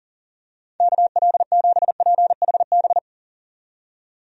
Morse code, heard as a keyed tone, about 40 wpm, 700 Hz. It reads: KL7PHB
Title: DXCC-100 Call Signs - 6 Characters - 40wpm